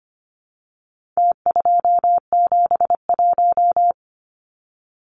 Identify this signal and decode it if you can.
Morse code, heard as a keyed tone, about 25 wpm, 700 Hz. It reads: T271
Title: Numbers - Contest Style - 4-Digit 25wpm